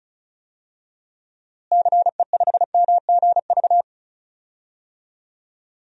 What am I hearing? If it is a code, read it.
Morse code, heard as a keyed tone, about 36 wpm, 700 Hz. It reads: CE5MGV
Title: DXCC-100 Call Signs - 6 Characters - Repeated 3x - 35wpm